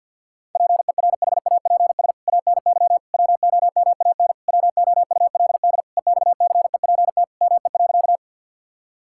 Morse code, heard as a keyed tone, about 51 wpm, 700 Hz. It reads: PERHAPS ANY WOMAN WOULD EXCEPT ME.
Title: Sentences from Top 1600 Words 50wpm